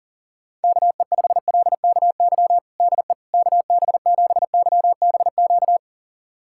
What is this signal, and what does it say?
Morse code, heard as a keyed tone, about 40 wpm, 700 Hz. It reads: KE5LKY DE KB7YBQ
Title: USA Callsign DE Callsign - 6 Characters - 40wpm